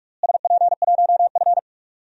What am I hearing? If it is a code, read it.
Morse code, heard as a keyed tone, about 43 wpm, 700 Hz. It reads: SP1F
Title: DXCC-100 Call Signs - 4 Characters - Repeated 2x - 45wpm